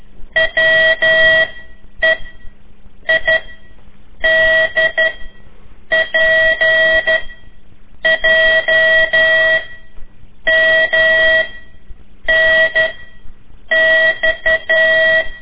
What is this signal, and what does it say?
Morse code, heard as a keyed tone, about 12 wpm, 1950 Hz. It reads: W E I D P J M N X
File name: ik1weg2 / CORSI CW / lez4-3